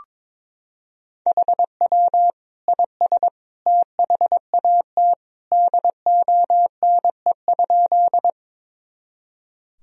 Morse code, heard as a keyed tone, about 22 wpm, 700 Hz. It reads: HW IS THAT DONE?
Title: Sentences from Top 500 Words and Abbreviations 22wpm